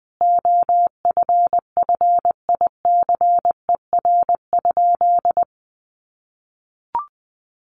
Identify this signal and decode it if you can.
Morse code, heard as a keyed tone, about 20 wpm, 700 Hz. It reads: OFFICER?
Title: Sentences from Top 1200 Words 20wpm